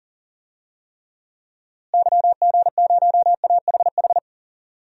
Morse code, heard as a keyed tone, about 40 wpm, 700 Hz. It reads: YG0AHH
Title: DXCC-100 Call Signs - 6 Characters - Repeated 3x - 40wpm